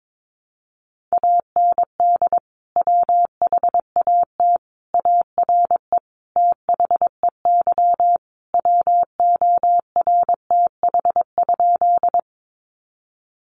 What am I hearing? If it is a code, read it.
Morse code, heard as a keyed tone, about 22 wpm, 700 Hz. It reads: AND WHAT ARE THEY WORTH?
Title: Sentences from Top 800 Words 22wpm